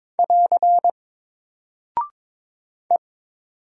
Morse code, heard as a keyed tone, about 22 wpm, 700 Hz. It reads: <AR> E
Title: Single Character (Letter:Number:Punctuation:Prosign) 22wpm